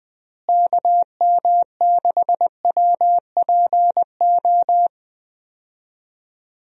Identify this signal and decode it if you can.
Morse code, heard as a keyed tone, about 20 wpm, 700 Hz. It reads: KM6WPO
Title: USA Call Signs - 6 Characters - Repeated 2x - 20wpm